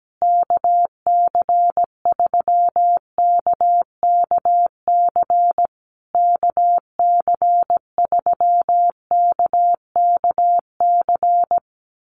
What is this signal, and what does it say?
Morse code, heard as a keyed tone, about 17 wpm, 700 Hz. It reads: KC3KKC KC3KKC
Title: USA Call Signs Repeated 3x 17wpm